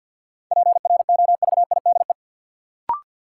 Morse code, heard as a keyed tone, about 46 wpm, 700 Hz. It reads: PROVIDE
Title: Sentences from Top 1000 Words - Review Part 2 - New Word Drill - 50wpm